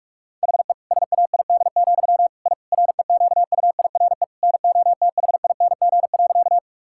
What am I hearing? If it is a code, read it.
Morse code, heard as a keyed tone, about 41 wpm, 700 Hz. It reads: HE SAID, I REQUIRE NOTHING.
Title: Sentences from Top 1000 Words - Review Part 6 - 45wpm